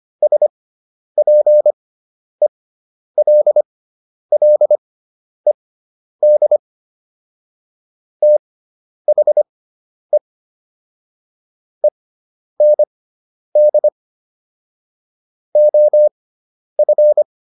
Morse code, heard as a keyed tone, about 25 wpm, spaced as for 10 wpm, 600 Hz. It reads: SPELLED THE END OF
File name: SS213 10